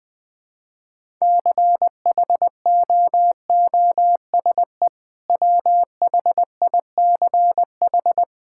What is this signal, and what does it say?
Morse code, heard as a keyed tone, about 20 wpm, 700 Hz. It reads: CHOOSE WHICH
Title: Sentences with 2-Word Phrases - Spoken Beginning - 20wpm